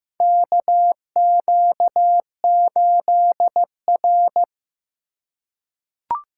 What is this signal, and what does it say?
Morse code, heard as a keyed tone, about 15 wpm, 700 Hz. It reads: KQ8R
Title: USA Call Signs - 2x Character Spacing - 15wpm